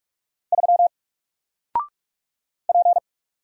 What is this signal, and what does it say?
Morse code, heard as a keyed tone, about 44 wpm, 700 Hz. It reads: 3 P
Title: Single Character (Letter:Number:Punctuation:Prosign) 45wpm